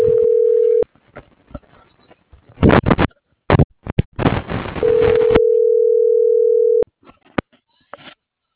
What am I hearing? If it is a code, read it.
Ringback tone (North America), 440 + 480 Hz.